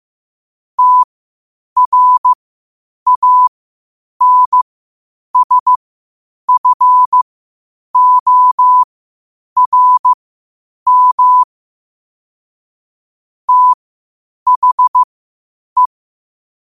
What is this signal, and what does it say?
Morse code, heard as a keyed tone, about 15 wpm, spaced as for 8 wpm, 1000 Hz. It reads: TRANSFORM THE
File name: qotdfiles / 8wpm